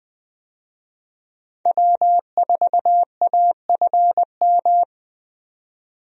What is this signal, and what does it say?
Morse code, heard as a keyed tone, about 21 wpm, 700 Hz. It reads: W4AFM
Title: USA Call Signs - 5 Characters - 20wpm